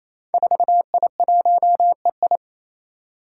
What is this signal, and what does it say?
Morse code, heard as a keyed tone, about 28 wpm, 700 Hz. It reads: 4I1EI
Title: Five Challenging Letter-Number Combinations 28wpm